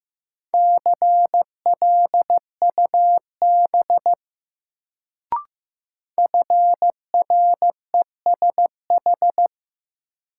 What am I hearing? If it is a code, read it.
Morse code, heard as a keyed tone, about 15 wpm, 700 Hz. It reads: CLUB FRESH
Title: Sentences from Top 1200 Words - New Word Drill Alt - 15wpm